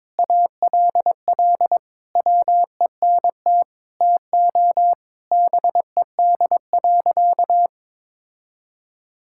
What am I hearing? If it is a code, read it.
Morse code, heard as a keyed tone, about 22 wpm, 700 Hz. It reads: ALL WENT TO BED.
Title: Sentences from Top 1000 Words - Review Part 6 - 22wpm